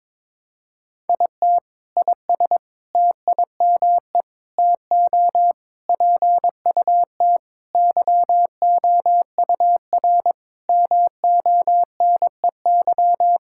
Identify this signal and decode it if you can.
Morse code, heard as a keyed tone, about 21 wpm, 700 Hz. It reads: IT IS TIME TO PUT YOUR MONEY
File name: Sentences with 7-Word Phrases - 22wpm